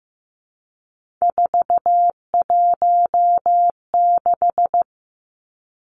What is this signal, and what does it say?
Morse code, heard as a keyed tone, about 15 wpm, 700 Hz. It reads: 416
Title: Numbers - 3-Digit 15wpm